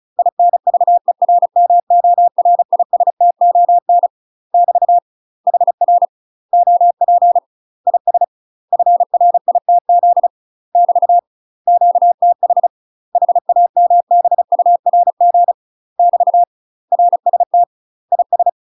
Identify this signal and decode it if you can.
Morse code, heard as a keyed tone, about 36 wpm, 700 Hz. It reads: INVERMORISTON <BT> HR OP IS FRITZ <BT> QTH HAMBURG <BT> RST IS
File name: UK MASTER QSOs TRAINING / 35WPM / UK MASTER QSOs 0026